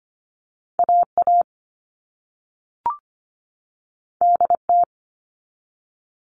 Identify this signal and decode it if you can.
Morse code, heard as a keyed tone, about 25 wpm, 700 Hz. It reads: AA DT
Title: All Cut Numbers - 2-Digit 25wpm